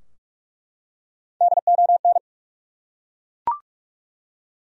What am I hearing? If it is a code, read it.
Morse code, heard as a keyed tone, about 46 wpm, 700 Hz. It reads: DON
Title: USA Common Names - Max 4 Letters - 45wpm